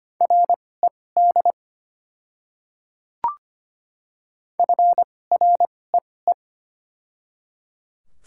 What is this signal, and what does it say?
Morse code, heard as a keyed tone, about 25 wpm, spaced as for 18 wpm, 700 Hz. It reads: RED FREE
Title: 2x Character Spacing - Top 500 Words - 25wpm